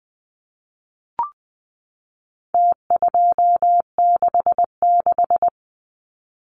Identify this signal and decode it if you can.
Morse code, heard as a keyed tone, about 20 wpm, 700 Hz. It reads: T266
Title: Numbers - Contest Style - 4-Digit 20wpm